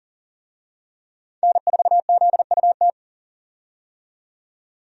Morse code, heard as a keyed tone, about 40 wpm, 700 Hz. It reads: N4ZUT
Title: USA Call Signs - 5 Characters - 40wpm